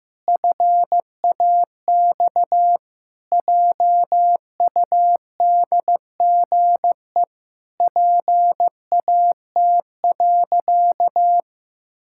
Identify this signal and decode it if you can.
Morse code, heard as a keyed tone, about 15 wpm, 700 Hz. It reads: FAX JUDGE PAT.
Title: Pangram Challenge v2 - 15wpm